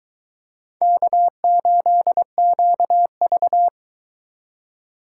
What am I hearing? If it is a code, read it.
Morse code, heard as a keyed tone, about 20 wpm, 700 Hz. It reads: K8QV